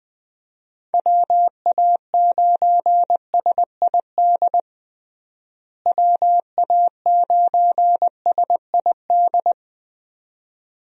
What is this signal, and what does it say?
Morse code, heard as a keyed tone, about 20 wpm, 700 Hz. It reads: WA9SID WA9SID
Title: USA Call Signs - 6 Characters - Repeated 2x - 20wpm